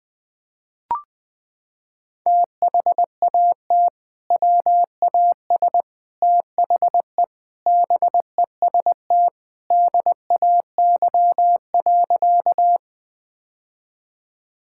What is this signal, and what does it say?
Morse code, heard as a keyed tone, about 21 wpm, 700 Hz. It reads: THAT WAS THE BEST DAY.
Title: Sentences with 3-Word Phrases - 20wpm